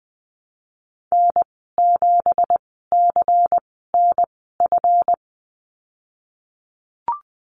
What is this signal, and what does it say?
Morse code, heard as a keyed tone, about 20 wpm, spaced as for 15 wpm, 700 Hz. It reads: N7CNF
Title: 179 Lesson 29g - Slash - Call Signs 20wpm